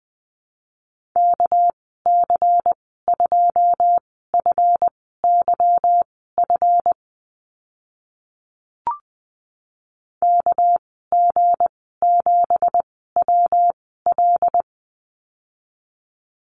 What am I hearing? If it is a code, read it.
Morse code, heard as a keyed tone, about 20 wpm, spaced as for 14 wpm, 700 Hz. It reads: KC2FYF KG7WL
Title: 207 Lesson 33g - Letter j - Call Signs 20wpm